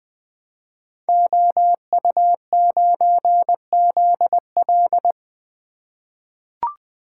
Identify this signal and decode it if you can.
Morse code, heard as a keyed tone, about 20 wpm, 700 Hz. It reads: OU9ZL
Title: Code Groups - 5 Letter:Numbers - 20wpm